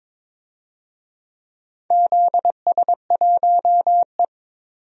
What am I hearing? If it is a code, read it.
Morse code, heard as a keyed tone, about 22 wpm, 700 Hz. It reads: ZS1E
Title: DXCC-100 Call Signs - 4 Characters - 22wpm